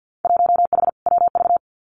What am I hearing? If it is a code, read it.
Morse code, heard as a keyed tone, about 50 wpm, 700 Hz. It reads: 1H WV
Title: Winter Field Day Exchanges Encore 50wpm